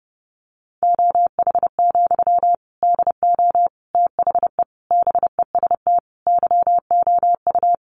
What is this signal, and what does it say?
Morse code, heard as a keyed tone, about 30 wpm, 700 Hz. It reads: OH, DO THE BEST YOU
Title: Sentences from Top 500 Words 30wpm